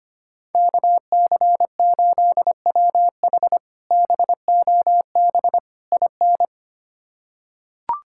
Morse code, heard as a keyed tone, about 25 wpm, 700 Hz. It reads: KC8WH BOB IN
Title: K1USN SST QSO Practice - 25wpm